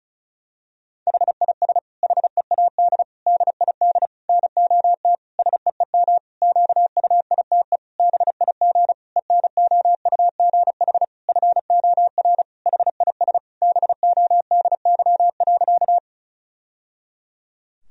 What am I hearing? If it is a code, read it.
Morse code, heard as a keyed tone, about 34 wpm, 700 Hz. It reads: HIS HEAD DID NOT SEEM QUITE BIG ENOUGH FOR HIS BODY.
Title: Sentences from Top 1000 Words - Review Part 2 - 35wpm